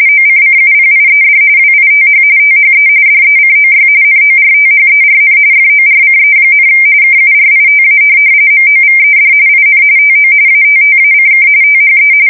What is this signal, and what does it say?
RTTY, about 45.45 baud. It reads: OCCUPIED WITH THE DECOY NORTHERN FORCE, KURITA MOVED IN TO ATTACK THE
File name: fsk170 / USS-Massachusetts